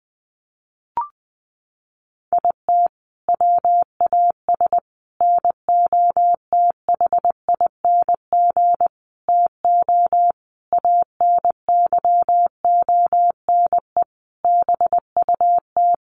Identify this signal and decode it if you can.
Morse code, heard as a keyed tone, about 19 wpm, 700 Hz. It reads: IT WAS NOTHING TO ANYONE BUT
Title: Sentences from Top 1300 Words 20wpm